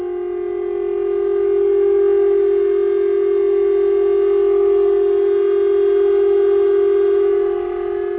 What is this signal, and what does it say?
Dial tone (Japan), 400 Hz.